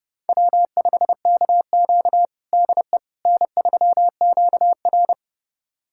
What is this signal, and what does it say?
Morse code, heard as a keyed tone, about 29 wpm, 700 Hz. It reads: W5KQ DE N3QR
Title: USA Callsign DE Callsign - 4 Characters - 30wpm